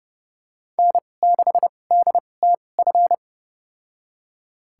Morse code, heard as a keyed tone, about 32 wpm, spaced as for 22 wpm, 700 Hz. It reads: N6DTF
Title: USA Call Signs - 5 Characters - 2x Character Spacing - 30wpm